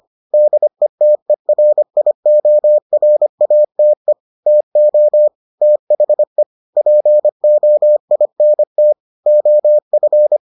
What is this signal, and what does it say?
Morse code, heard as a keyed tone, about 25 wpm, 600 Hz. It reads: DETERIORATE TO THE POINT OF